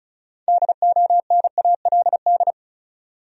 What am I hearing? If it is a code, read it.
Morse code, heard as a keyed tone, about 36 wpm, 700 Hz. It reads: DONALD
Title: USA Common Names - 35wpm